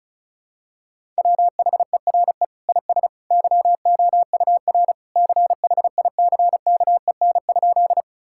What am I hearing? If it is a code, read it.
Morse code, heard as a keyed tone, about 34 wpm, 700 Hz. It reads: WHERE IS YOUR CHICKEN?
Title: Sentences from Top 1000 Words - Review Part 5 - 35wpm